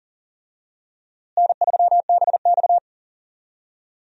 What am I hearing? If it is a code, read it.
Morse code, heard as a keyed tone, about 40 wpm, 700 Hz. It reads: N3BX
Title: USA Call Signs - 4 Characters - Repeated 2x - 40wpm